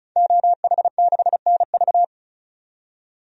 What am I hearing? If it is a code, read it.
Morse code, heard as a keyed tone, about 34 wpm, 700 Hz. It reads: OH6NV
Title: DXCC-100 Call Signs - 5 Characters - 35wpm